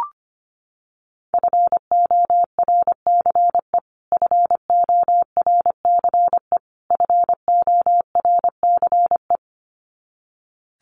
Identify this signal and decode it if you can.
Morse code, heard as a keyed tone, about 25 wpm, 700 Hz. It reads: FORCE FORCE FORCE
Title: Top 500 Words Repeated 25wpm